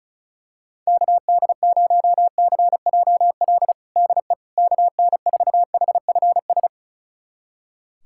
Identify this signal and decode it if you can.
Morse code, heard as a keyed tone, about 36 wpm, 700 Hz. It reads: KD0CJL DE KN4HFS
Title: USA Callsign DE Callsign 35wpm